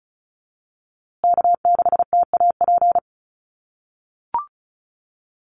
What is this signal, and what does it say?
Morse code, heard as a keyed tone, about 34 wpm, 700 Hz. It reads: K6TAP
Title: USA Call Signs Repeated 2x 35wpm